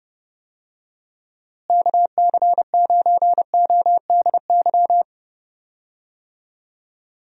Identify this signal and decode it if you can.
Morse code, heard as a keyed tone, about 30 wpm, 700 Hz. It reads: KC9ODY
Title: USA Call Signs - 6 Characters - Repeated 3x - 30wpm